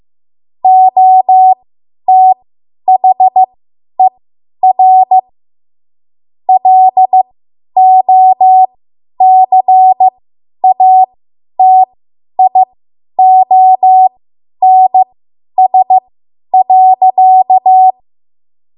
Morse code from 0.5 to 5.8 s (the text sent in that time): OTHER